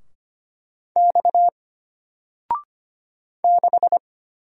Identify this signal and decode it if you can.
Morse code, heard as a keyed tone, about 25 wpm, 700 Hz. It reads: X 6
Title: Single Letter-Number Alt 25wpm